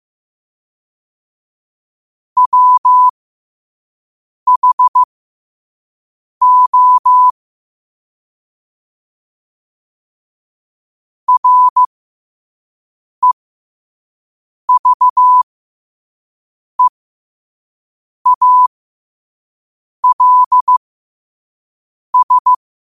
Morse code, heard as a keyed tone, about 15 wpm, spaced as for 5 wpm, 1000 Hz. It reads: WHO REVEALS
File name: qotdfiles / 5wpm